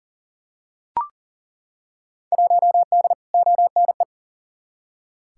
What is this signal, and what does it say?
Morse code, heard as a keyed tone, about 40 wpm, 700 Hz. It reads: 1D ONE
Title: Field Day Exchanges - 40wpm